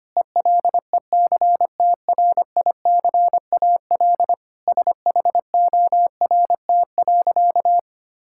Morse code, heard as a keyed tone, about 25 wpm, 700 Hz. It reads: ELECTRICAL SHORT.
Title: Sentences from Top 1000 Words - Review Part 6 - 25wpm